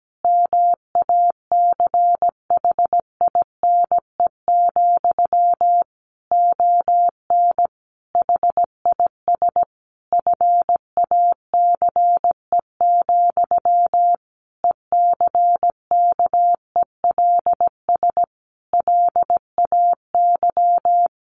Morse code, heard as a keyed tone, about 17 wpm, 700 Hz. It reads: MACHINE, ON HIS FACE, ECKELS LAY
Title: A Sound Of Thunder 17wpm